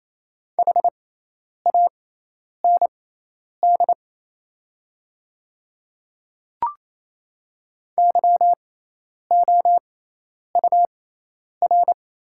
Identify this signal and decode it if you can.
Morse code, heard as a keyed tone, about 28 wpm, spaced as for 10 wpm, 700 Hz. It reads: HAND YOUR
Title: 6x Character Spacing - Top 300 Words - 28wpm